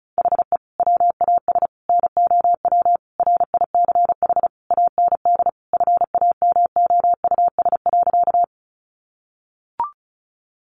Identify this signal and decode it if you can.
Morse code, heard as a keyed tone, about 34 wpm, 700 Hz. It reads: HE WAS NOW RICH AND FAMOUS.
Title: Sentences from Top 1500 Words 35wpm